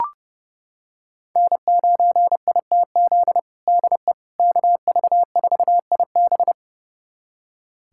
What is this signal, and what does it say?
Morse code, heard as a keyed tone, about 30 wpm, 700 Hz. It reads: N9ITZ DE KV4IB
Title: USA Callsign DE Callsign - 5 Characters - 30wpm